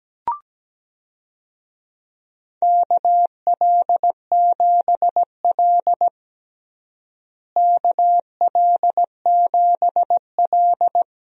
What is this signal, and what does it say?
Morse code, heard as a keyed tone, about 17 wpm, 700 Hz. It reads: KL7L KL7L
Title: DXCC-100 Call Signs - 4 Characters - Repeated 3x - 17wpm